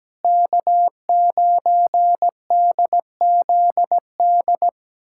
Morse code, heard as a keyed tone, about 17 wpm, 700 Hz. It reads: K9DZD
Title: USA Callsign DE Callsign - 5 Characters - 17wpm